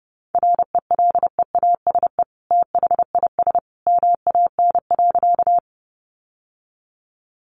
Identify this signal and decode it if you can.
Morse code, heard as a keyed tone, about 30 wpm, 700 Hz. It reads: RELEASE THIS MAN.
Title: Sentences from Top 1100 Words 30wpm